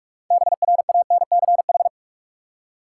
Morse code, heard as a keyed tone, about 43 wpm, 700 Hz. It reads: BRANCH
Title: Six Letter Words - 45wpm